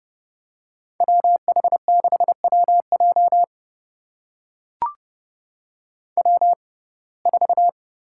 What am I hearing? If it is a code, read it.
Morse code, heard as a keyed tone, about 30 wpm, spaced as for 22 wpm, 700 Hz. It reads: WH6WJ W4
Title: USA Call Signs - 5 Characters - 6x Character Spacing - 30wpm